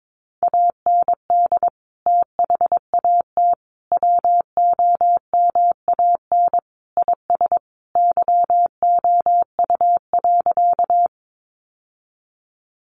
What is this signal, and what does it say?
Morse code, heard as a keyed tone, about 22 wpm, 700 Hz. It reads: AND THAT WOMAN IS YOU.
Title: Sentences from Top 1100 Words 22wpm